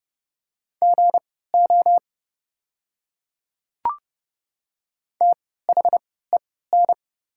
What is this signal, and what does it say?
Morse code, heard as a keyed tone, about 30 wpm, spaced as for 17 wpm, 690 Hz. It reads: GO THEN
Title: 3x Character Spacing - Top 100 Words - 30wpm